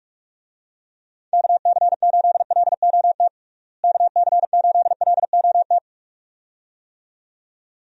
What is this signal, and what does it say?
Morse code, heard as a keyed tone, about 45 wpm, 700 Hz. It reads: KC8LOT KC8LOT
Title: USA Call Signs - 6 Characters - Repeated 2x - 45wpm